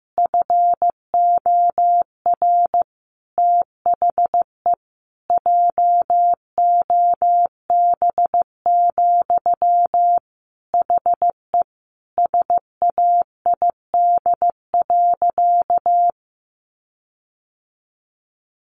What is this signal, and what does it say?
Morse code, heard as a keyed tone, about 15 wpm, 700 Hz. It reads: FOR THE JOB, HE SAID.
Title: Sentences from Top 1500 Words 15wpm